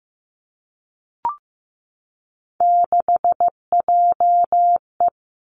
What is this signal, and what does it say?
Morse code, heard as a keyed tone, about 15 wpm, 700 Hz. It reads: 6JE
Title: Three Challenging Letter-Number Combinations 15wpm